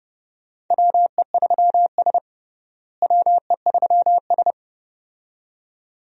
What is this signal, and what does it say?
Morse code, heard as a keyed tone, about 30 wpm, 700 Hz. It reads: WE3S WE3S
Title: USA Call Signs - 4 Characters - Repeated 2x - 30wpm